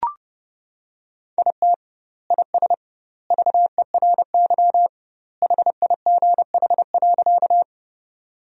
Morse code, heard as a keyed tone, about 32 wpm, 700 Hz. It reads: IT IS VERY HIGH.
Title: 2x Word-Spacing - Sentences from Top 200 Words - 30wpm